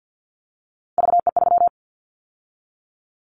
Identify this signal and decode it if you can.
Morse code, heard as a keyed tone, about 50 wpm, 700 Hz. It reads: VE3
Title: Callsign Prefixes for Top 25 DXCC - 50wpm